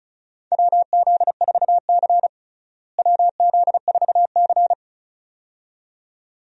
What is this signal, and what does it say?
Morse code, heard as a keyed tone, about 35 wpm, 700 Hz. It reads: WZ4C WZ4C
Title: USA Call Signs - 4 Characters - Repeated 2x - 35wpm